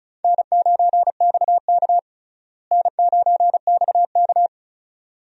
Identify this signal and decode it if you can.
Morse code, heard as a keyed tone, about 36 wpm, 700 Hz. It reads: N9XK N9XK
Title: USA Call Signs - 4 Characters - Repeated 2x - 35wpm